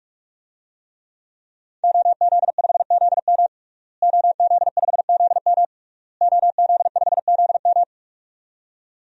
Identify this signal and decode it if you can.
Morse code, heard as a keyed tone, about 41 wpm, 700 Hz. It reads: OZ5ZM OZ5ZM OZ5ZM
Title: DXCC-100 Call Signs - 5 Characters - Repeated 3x - 45wpm